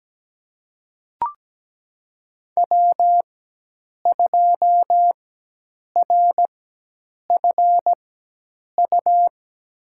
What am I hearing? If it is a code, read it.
Morse code, heard as a keyed tone, about 17 wpm, spaced as for 8 wpm, 700 Hz. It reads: W2RFU
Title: USA Call Signs - 5 Characters - 4x Character Spacing - 17wpm